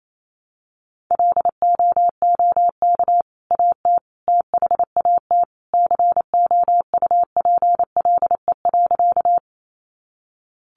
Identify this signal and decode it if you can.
Morse code, heard as a keyed tone, about 28 wpm, 700 Hz. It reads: LOOK AT THAT COUPLE.
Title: Sentences from Top 1100 Words 28wpm